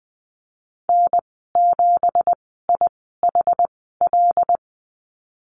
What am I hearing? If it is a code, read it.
Morse code, heard as a keyed tone, about 20 wpm, spaced as for 15 wpm, 700 Hz. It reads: N7IHL
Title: 186 Lesson 30g - Letter b - Call Signs 20wpm